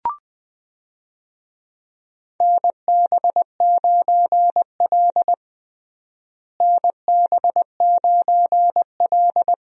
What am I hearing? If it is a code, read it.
Morse code, heard as a keyed tone, about 20 wpm, 700 Hz. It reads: NB9L NB9L
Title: USA Call Signs - 4 Characters - Repeated 2x - 20wpm